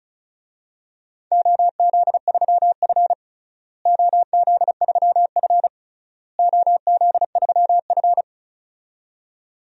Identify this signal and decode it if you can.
Morse code, heard as a keyed tone, about 36 wpm, 690 Hz. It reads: OZ3F OZ3F OZ3F
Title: DXCC-100 Call Signs - 4 Characters - Repeated 3x - 35wpm